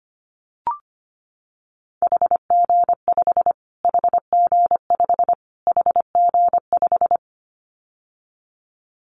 Morse code, heard as a keyed tone, about 25 wpm, 700 Hz. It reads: HG5 HG5 HG5
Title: Callsign Prefixes for Top 25 DXCC - Repeated 3x - 25wpm